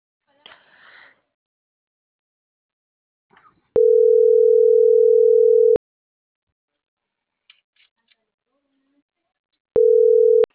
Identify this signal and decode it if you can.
Ringback tone (North America), 440 + 480 Hz.